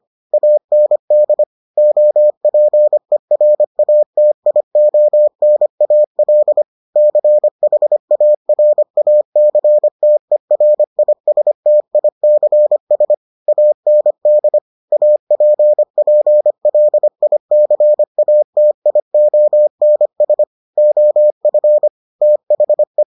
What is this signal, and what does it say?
Morse code, heard as a keyed tone, about 25 wpm, 600 Hz. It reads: AND OPERATIONAL CHARACTERISTICS AND APPLICATIONS OF THE